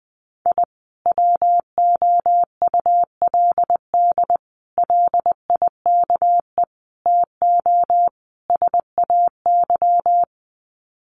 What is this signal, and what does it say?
Morse code, heard as a keyed tone, about 20 wpm, 700 Hz. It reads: I WOULD LIKE TO SAY
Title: Sets of 5 Words Encore2 20wpm